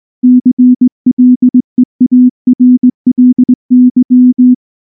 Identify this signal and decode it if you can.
Morse code, heard as a keyed tone, about 19 wpm, 250 Hz. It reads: CLEARLY
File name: morse / morse-training-20WPM-1